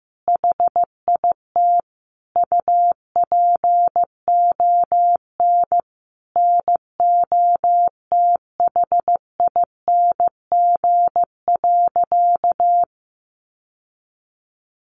Morse code, heard as a keyed tone, about 14 wpm, 700 Hz. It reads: HIT UPON NOTHING.
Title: Sentences from Top 1200 Words 15wpm